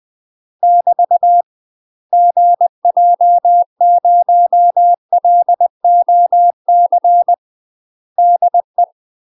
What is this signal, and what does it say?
Morse code, heard as a keyed tone, about 20 wpm, 700 Hz. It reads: <BT> GJ0LOC DE
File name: UK MASTER QSOs TRAINING / 20WPM / UK MASTER QSOs 0058